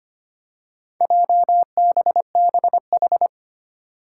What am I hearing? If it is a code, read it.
Morse code, heard as a keyed tone, about 25 wpm, 700 Hz. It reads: JBBH
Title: Four Challenging Letter-Number Combinations 25wpm